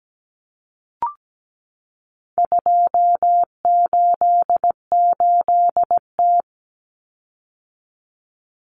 Morse code, heard as a keyed tone, about 17 wpm, 700 Hz. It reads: 288T
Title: Common Cut Numbers - 4-Digit 17wpm